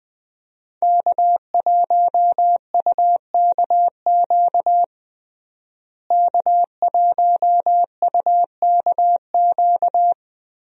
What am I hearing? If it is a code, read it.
Morse code, heard as a keyed tone, about 20 wpm, 700 Hz. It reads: K1UKQ K1UKQ
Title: USA Call Signs - 5 Characters - Repeated 2x - 20wpm